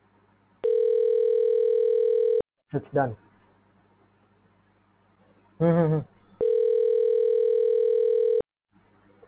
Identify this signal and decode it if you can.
Ringback tone (North America), 440 + 480 Hz, with speech.
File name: recordings / REa91adf6ef7a64fb8ad03147f0ed28c2a